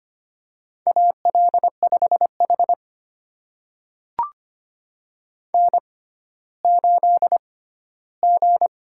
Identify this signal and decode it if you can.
Morse code, heard as a keyed tone, about 25 wpm, spaced as for 14 wpm, 700 Hz. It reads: AL5H N8G
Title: USA Call Signs - 6x Character Spacing - 25wpm